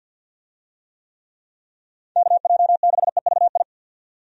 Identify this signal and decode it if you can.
Morse code, heard as a keyed tone, about 53 wpm, 700 Hz. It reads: KJ6EVI
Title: USA Call Signs - 6 Characters - 50wpm